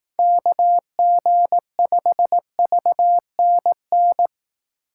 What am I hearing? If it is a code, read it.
Morse code, heard as a keyed tone, about 18 wpm, 700 Hz. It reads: KG5VNN
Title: Speed Racing - USA 2x3 Call Signs 15wpm